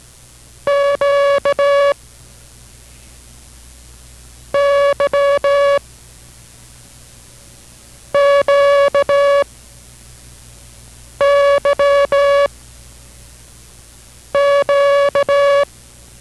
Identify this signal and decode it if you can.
Morse code, heard as a keyed tone, about 17 wpm, 560 Hz. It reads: Q Y Q Y Q